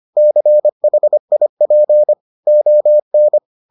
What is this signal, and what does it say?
Morse code, heard as a keyed tone, about 25 wpm, 600 Hz. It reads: CHIP ON